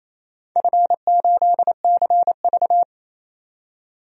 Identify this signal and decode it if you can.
Morse code, heard as a keyed tone, about 28 wpm, 700 Hz. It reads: F8CV
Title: DXCC-100 Call Signs - 4 Characters - 28wpm